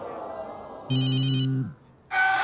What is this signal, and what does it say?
Speech. There is no code.